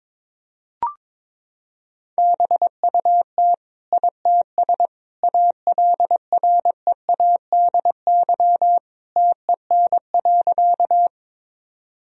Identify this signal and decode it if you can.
Morse code, heard as a keyed tone, about 22 wpm, 700 Hz. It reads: BUT ITS ALREADY TEN.
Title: Sentences from Top 1000 Words - Review Part 2 - 22wpm